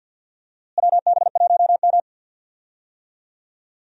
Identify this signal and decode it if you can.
Morse code, heard as a keyed tone, about 51 wpm, 700 Hz. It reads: WB1M
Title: USA Call Signs - 4 Characters - Repeated 2x - 50wpm